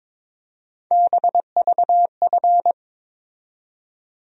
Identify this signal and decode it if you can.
Morse code, heard as a keyed tone, about 22 wpm, 700 Hz. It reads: BVF
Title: USA Call Sign Postfixes - 3 Letters 22wpm